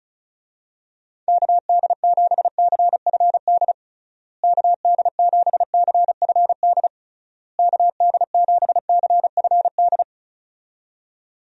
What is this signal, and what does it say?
Morse code, heard as a keyed tone, about 36 wpm, 700 Hz. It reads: KD7CFD KD7CFD KD7CFD
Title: USA Call Signs - 6 Characters - Repeated 3x - 35wpm